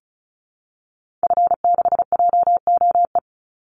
Morse code, heard as a keyed tone, about 34 wpm, 700 Hz. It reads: F6JOE
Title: Call Signs Repeated 3x - 35wpm